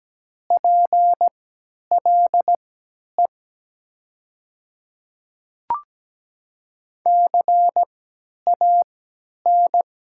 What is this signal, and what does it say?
Morse code, heard as a keyed tone, about 17 wpm, spaced as for 10 wpm, 700 Hz. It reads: PLE CAN
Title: 3x Character Spacing - Top 100 Words - 17wpm